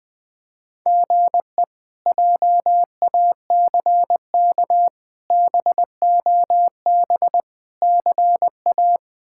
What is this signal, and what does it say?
Morse code, heard as a keyed tone, about 20 wpm, 700 Hz. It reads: GE JACK BOB CA
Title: K1USN SST QSO Practice - 20wpm